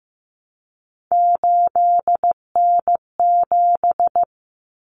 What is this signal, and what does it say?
Morse code, heard as a keyed tone, about 15 wpm, 700 Hz. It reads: 8N7
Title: Common Cut Numbers - 3-Digit 15wpm